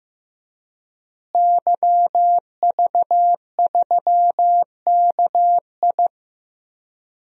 Morse code, heard as a keyed tone, about 15 wpm, 700 Hz. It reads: YV3KI
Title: DXCC-100 Call Signs - 5 Characters - Repeated 2x - 15wpm